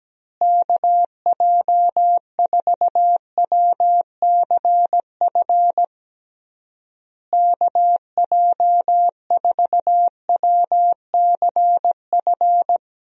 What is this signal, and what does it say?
Morse code, heard as a keyed tone, about 17 wpm, 700 Hz. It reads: KJ4WCF KJ4WCF
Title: USA Call Signs - 6 Characters - Repeated 3x - 17wpm